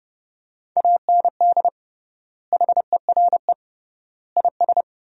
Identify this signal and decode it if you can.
Morse code, heard as a keyed tone, about 30 wpm, 700 Hz. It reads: AND HERE IS
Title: 3x Word-Spacing - Sentences from Top 500 Words - 30wpm